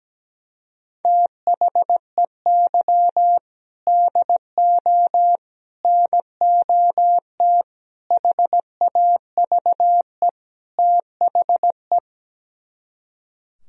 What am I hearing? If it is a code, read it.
Morse code, heard as a keyed tone, about 17 wpm, 700 Hz. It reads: THEY DO NOT HAVE THE
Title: Speed-Racing - Sets of 5 Words 17wpm